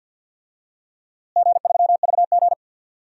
Morse code, heard as a keyed tone, about 52 wpm, 700 Hz. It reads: G3VG
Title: DXCC-100 Call Signs - 4 Characters - Repeated 2x - 50wpm